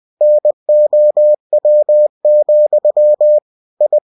Morse code, heard as a keyed tone, about 20 wpm, 600 Hz. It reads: NOW, I
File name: cw-books / winnie-the-pooh / winnie-20wpm-page-0034